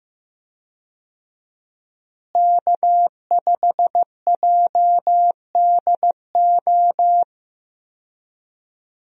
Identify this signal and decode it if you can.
Morse code, heard as a keyed tone, about 15 wpm, 700 Hz. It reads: K5JDO
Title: USA Call Signs - 5 Characters - Repeated 3x - 15wpm